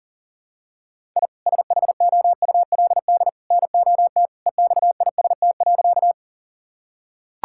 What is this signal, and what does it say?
Morse code, heard as a keyed tone, about 40 wpm, 700 Hz. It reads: I SHOULD NOT EXIST.
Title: Sentences from Top 1000 Words - Review Part 5 - 40wpm